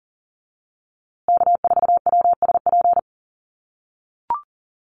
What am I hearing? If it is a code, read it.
Morse code, heard as a keyed tone, about 40 wpm, 700 Hz. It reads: K4WSP
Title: USA Call Signs Repeated 2x 40wpm